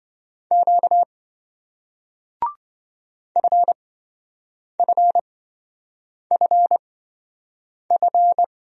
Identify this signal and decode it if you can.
Morse code, heard as a keyed tone, about 27 wpm, spaced as for 7 wpm, 700 Hz. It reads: Q FFFF